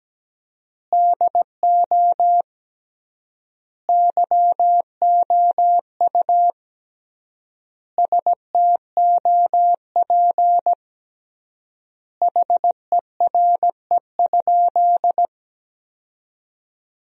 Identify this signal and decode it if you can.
Morse code, heard as a keyed tone, about 17 wpm, 700 Hz. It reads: DO YOU STOP HERE?
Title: 3x Word-Spacing - Sentences from Top 300 Words - 17wpm